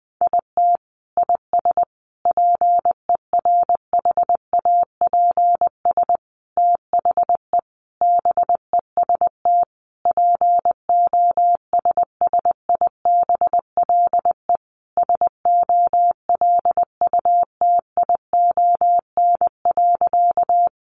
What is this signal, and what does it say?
Morse code, heard as a keyed tone, about 20 wpm, 700 Hz. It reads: IT IS PERHAPS THE BEST POSSIBLE SOLUTION.
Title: Sentences from Top 1300 Words 20wpm